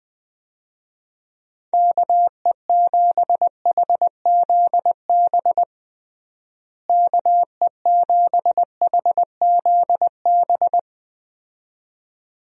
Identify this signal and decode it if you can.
Morse code, heard as a keyed tone, about 20 wpm, 700 Hz. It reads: KE7HZB KE7HZB
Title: USA Call Signs - 6 Characters - Repeated 2x - 20wpm